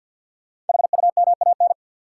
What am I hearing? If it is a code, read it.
Morse code, heard as a keyed tone, about 51 wpm, 700 Hz. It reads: HUMAN
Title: Sets of 5 Words - New Word Drill - 50wpm